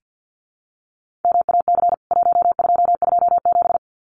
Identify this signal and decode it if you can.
Morse code, heard as a keyed tone, about 48 wpm, 700 Hz. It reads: MAC 1227
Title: CWT Exchange 50wpm